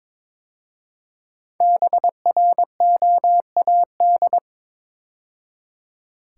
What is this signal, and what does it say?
Morse code, heard as a keyed tone, about 22 wpm, 690 Hz. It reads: BROAD
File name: 5 Letter Words - First 3 Letters Given - 22wpm